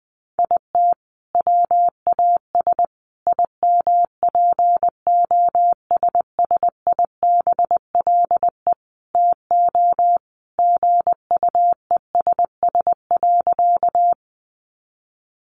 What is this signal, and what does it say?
Morse code, heard as a keyed tone, about 20 wpm, 700 Hz. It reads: IT WAS IMPOSSIBLE TO GUESS.
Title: Sentences from Top 1500 Words 20wpm